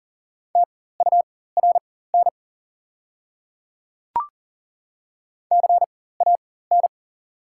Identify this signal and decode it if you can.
Morse code, heard as a keyed tone, about 40 wpm, spaced as for 19 wpm, 690 Hz. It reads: TURN CAN
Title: 4x Character Spacing - Top 200 Words - 40wpm